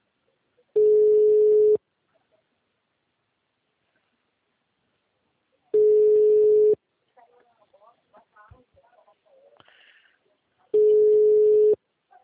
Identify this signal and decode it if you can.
Ringback tone (Europe, 425 Hz), with speech.